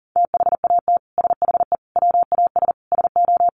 Morse code, heard as a keyed tone, about 40 wpm, 700 Hz. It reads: THAT SHE WAS SO
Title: Sentences from Top 1400 Words 40wpm